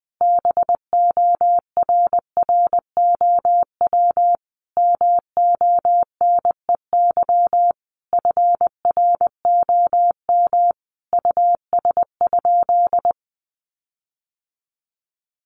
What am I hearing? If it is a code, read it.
Morse code, heard as a keyed tone, about 20 wpm, 700 Hz. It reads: BORROW MONEY FROM US?
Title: Sentences from Top 1500 Words 20wpm